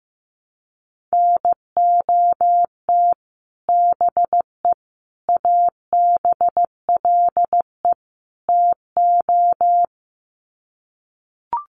Morse code, heard as a keyed tone, about 15 wpm, 700 Hz. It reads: NOT BE ABLE TO
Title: Sets of 4 Words 15wpm